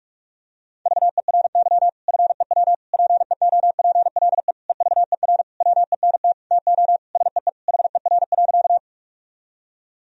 Morse code, heard as a keyed tone, about 46 wpm, 700 Hz. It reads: VERY FEW PEOPLE EVER WENT TO SEE HER.
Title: Sentences from Top 1000 Words - Review Part 1 - 45wpm